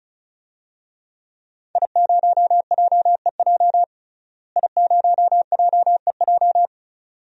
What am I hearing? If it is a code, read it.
Morse code, heard as a keyed tone, about 35 wpm, 700 Hz. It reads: I0JEJ I0JEJ
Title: DXCC-100 Call Signs - 5 Characters - Repeated 2x - 35wpm